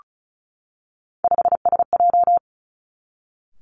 Morse code, heard as a keyed tone, about 36 wpm, 700 Hz. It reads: 5SJ
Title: Three Challenging Letter-Number Combinations 35wpm